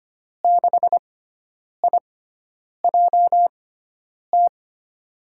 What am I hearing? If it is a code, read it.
Morse code, heard as a keyed tone, about 25 wpm, spaced as for 9 wpm, 700 Hz. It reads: 6IJT
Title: USA Call Signs - 5 Characters - 6x Character Spacing - 25wpm